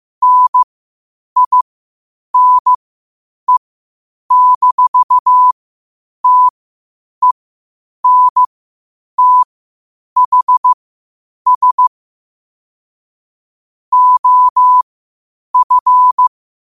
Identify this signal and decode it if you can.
Morse code, heard as a keyed tone, about 15 wpm, spaced as for 8 wpm, 1000 Hz. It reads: NINE-TENTHS OF
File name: qotdfiles / 8wpm